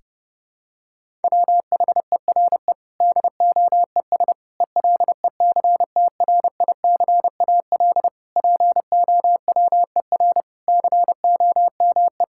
Morse code, heard as a keyed tone, about 30 wpm, 700 Hz. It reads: WHERE DOES ELECTRICAL POWER COME
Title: Sentences from Top 1000 Words - Review Part 6 - 30wpm